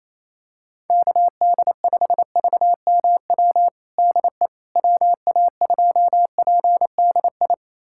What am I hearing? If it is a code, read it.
Morse code, heard as a keyed tone, about 28 wpm, 700 Hz. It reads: KD5VMW DE WA2PDI
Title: USA Callsign DE Callsign - 6 Characters - 28wpm